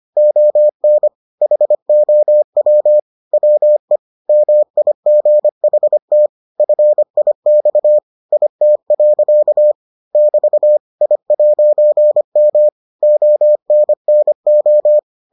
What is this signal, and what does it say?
Morse code, heard as a keyed tone, about 25 wpm, 600 Hz. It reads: ON HOW WE MIGHT FIX IT. <BT> I'M ONNO